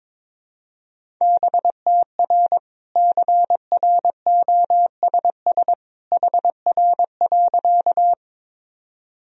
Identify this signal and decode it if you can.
Morse code, heard as a keyed tone, about 21 wpm, 700 Hz. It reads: BTR CROSS HR.
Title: Sentences from Top 400 Words and Abbreviations 22wpm